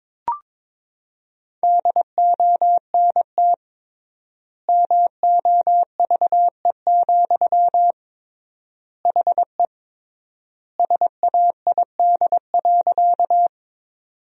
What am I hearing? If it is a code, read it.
Morse code, heard as a keyed tone, about 22 wpm, 700 Hz. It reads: DONT MOVE, HE SAID.
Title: 3x Word-Spacing - Sentences from Top 300 Words - 22wpm